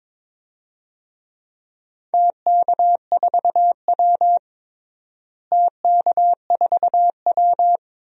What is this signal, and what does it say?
Morse code, heard as a keyed tone, about 22 wpm, 700 Hz. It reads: TK4W TK4W
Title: DXCC-100 Call Signs - 4 Characters - Repeated 2x - 22wpm